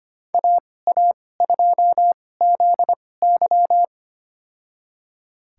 Morse code, heard as a keyed tone, about 25 wpm, spaced as for 18 wpm, 700 Hz. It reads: AA2ZY
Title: USA Call Signs - 5 Characters - 2x Character Spacing - 25wpm